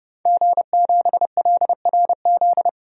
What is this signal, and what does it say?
Morse code, heard as a keyed tone, about 30 wpm, 700 Hz. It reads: G7LRZ
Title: DXCC-100 Call Signs - 5 Characters - Repeated 2x - 30wpm